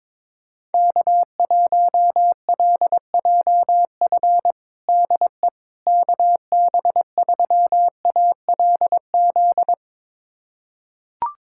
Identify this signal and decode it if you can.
Morse code, heard as a keyed tone, about 22 wpm, 700 Hz. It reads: K1LJF DE KB3ALZ
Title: USA Callsign DE Callsign 22wpm